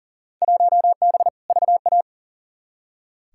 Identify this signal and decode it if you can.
Morse code, heard as a keyed tone, about 40 wpm, 700 Hz. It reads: 1B VA
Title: Field Day Exchanges Encore - 40wpm